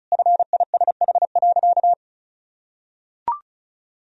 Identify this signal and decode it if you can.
Morse code, heard as a keyed tone, about 36 wpm, 700 Hz. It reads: FISH.
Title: Sentences from Top 1000 Words - Review Part 1 - 35wpm